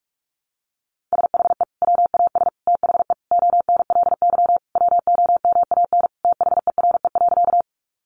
Morse code, heard as a keyed tone, about 44 wpm, 700 Hz. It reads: SHE WAS THE ONLY WOMAN THERE.
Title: Sentences from Top 1100 Words 45wpm